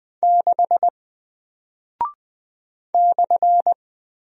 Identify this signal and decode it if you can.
Morse code, heard as a keyed tone, about 20 wpm, 700 Hz. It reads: 6 /
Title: Single Character (Letter:Number:Punctuation:Prosign) Alt 20wpm